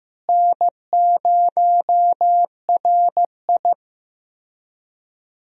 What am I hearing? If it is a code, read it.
Morse code, heard as a keyed tone, about 15 wpm, 700 Hz. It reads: N0RI
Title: USA Call Signs - 4 Characters - Repeated 3x - 15wpm